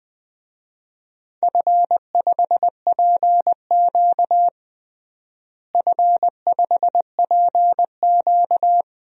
Morse code, heard as a keyed tone, about 20 wpm, 700 Hz. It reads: F5PQ F5PQ
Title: DXCC-100 Call Signs - 4 Characters - Repeated 2x - 20wpm